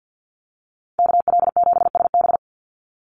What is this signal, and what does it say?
Morse code, heard as a keyed tone, about 50 wpm, 700 Hz. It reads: KL7SB
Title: Call Signs - 50wpm